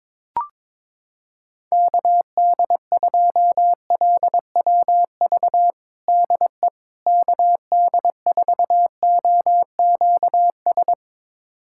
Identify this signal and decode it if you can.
Morse code, heard as a keyed tone, about 22 wpm, 700 Hz. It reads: KD2LWV DE KD4OQS
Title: USA Callsign DE Callsign - 6 Characters - 22wpm